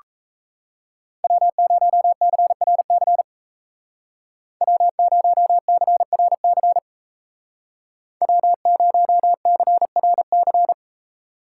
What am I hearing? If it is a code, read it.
Morse code, heard as a keyed tone, about 37 wpm, 700 Hz. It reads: W0CRC W0CRC W0CRC